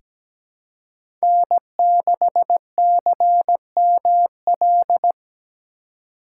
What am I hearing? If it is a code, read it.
Morse code, heard as a keyed tone, about 17 wpm, 700 Hz. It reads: N6CML
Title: USA Call Signs - 5 Characters - 2x Character Spacing - 17wpm